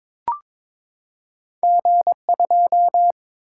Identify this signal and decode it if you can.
Morse code, heard as a keyed tone, about 22 wpm, 700 Hz. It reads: G2
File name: practice / 22wpm / Two Letter-Number Combinations - 22wpm